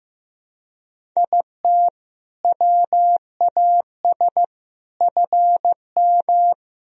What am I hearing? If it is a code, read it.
Morse code, heard as a keyed tone, about 15 wpm, 700 Hz. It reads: IT WAS FM
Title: Sentences from Top 100 Words and Abbreviations 15wpm